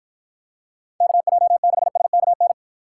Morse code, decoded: KJ6SKN

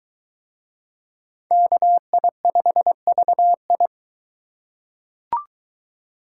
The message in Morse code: KI5VI

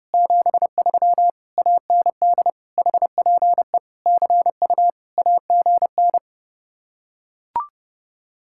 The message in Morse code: 73 AND HPE CU AGN

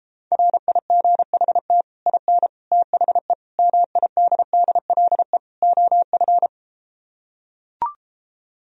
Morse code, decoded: RIGHT IN THE MIDDLE OF

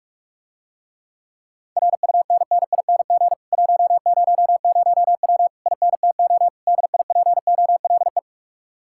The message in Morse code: RUNNING 100W INTO DIPOLE

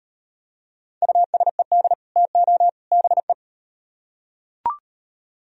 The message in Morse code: USED TO BE

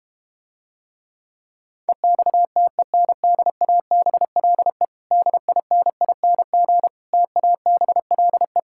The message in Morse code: EXTENDABLE DINING TABLE

